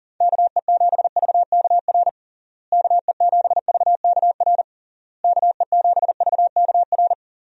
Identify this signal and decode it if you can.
Morse code, heard as a keyed tone, about 40 wpm, 700 Hz. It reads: KE7VKR KE7VKR KE7VKR
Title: USA Call Signs - 6 Characters - Repeated 3x - 40wpm